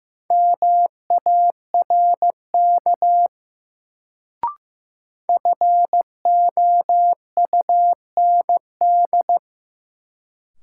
Morse code, heard as a keyed tone, about 15 wpm, 700 Hz. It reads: MARK FOUND
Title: Sentences from Top 300 Words - New Word Drill - 15wpm